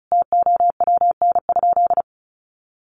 Morse code, heard as a keyed tone, about 34 wpm, 700 Hz. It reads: TOWN?
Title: Sentences from Top 1300 Words 35wpm